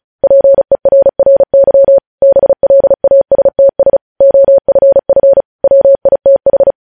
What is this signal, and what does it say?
Morse code, heard as a keyed tone, about 36 wpm, 550 Hz. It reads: PERRY BLASTS OFF WITH
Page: news headlines in morse code at 35 wpm